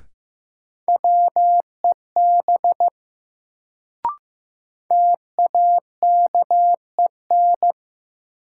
Morse code, heard as a keyed tone, about 15 wpm, 700 Hz. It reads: WEB TAKEN
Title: Sentences from Top 1000 Words - Review Part 4 - New Word Drill - 15wpm